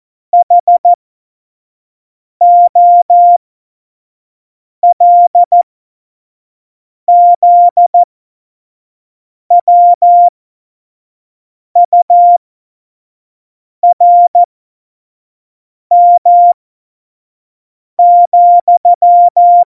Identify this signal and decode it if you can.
Morse code, heard as a keyed tone, about 14 wpm, spaced as for 5 wpm, 700 Hz. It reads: HOLZWURM,
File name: Morsekurs / Witze / Witz 11. September 2018, 25 F70